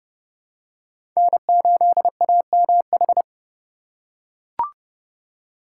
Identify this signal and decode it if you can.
Morse code, heard as a keyed tone, about 30 wpm, 700 Hz. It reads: N8AMH
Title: USA Call Signs - 4x Character Spacing - 30wpm